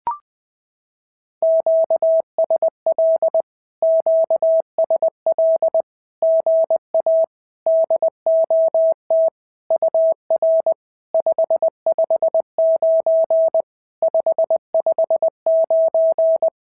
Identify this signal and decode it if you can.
Morse code, heard as a keyed tone, about 20 wpm, 650 Hz. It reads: QSL QSL GA DOT UR 559 559
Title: SOTA QSO Practice 20wpm